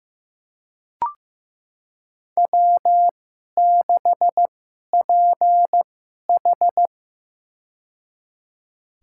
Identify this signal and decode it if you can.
Morse code, heard as a keyed tone, about 15 wpm, spaced as for 11 wpm, 700 Hz. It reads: W6PH
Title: USA Call Signs - 2x Character Spacing - 15wpm